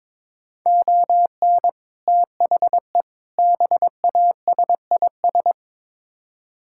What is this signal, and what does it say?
Morse code, heard as a keyed tone, about 21 wpm, 700 Hz. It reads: ON THE BASIS
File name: Build A Phrase - 5-Word Phrases - 22wpm